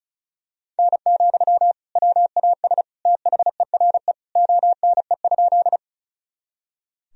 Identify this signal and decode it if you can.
Morse code, heard as a keyed tone, about 34 wpm, 700 Hz. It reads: N, WAS THERE ONE?
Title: Sentences from Top 100 Words and Abbreviations 35wpm